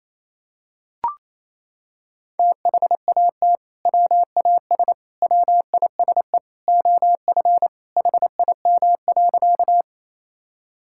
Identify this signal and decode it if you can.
Morse code, heard as a keyed tone, about 28 wpm, 700 Hz. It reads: THAT WAS WISE OF HIM.
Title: Sentences from Top 1600 Words 28wpm